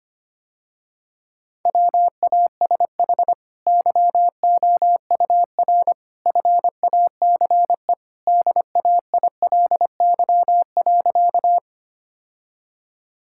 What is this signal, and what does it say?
Morse code, heard as a keyed tone, about 25 wpm, 700 Hz. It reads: WASH YOUR FACE DAILY.
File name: Easy Sentences with Hints - 25wpm